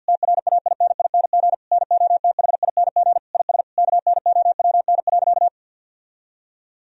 Morse code, heard as a keyed tone, about 50 wpm, 700 Hz. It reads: TRAINING NOTHING IS KNOWN.
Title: Sentences from Top 1000 Words - Review Part 4 - 50wpm